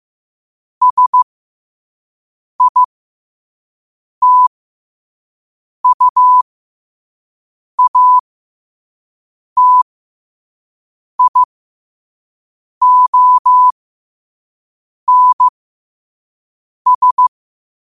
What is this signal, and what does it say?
Morse code, heard as a keyed tone, about 15 wpm, spaced as for 5 wpm, 1000 Hz. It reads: SITUATIONS
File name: qotdfiles / 5wpm